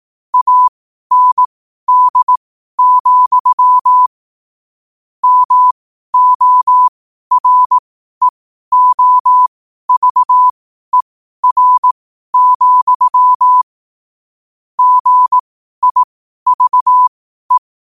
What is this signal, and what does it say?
Morse code, heard as a keyed tone, about 18 wpm, spaced as for 12 wpm, 1000 Hz. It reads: AND, MOREOVER, GIVE
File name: qotdfiles / 12wpm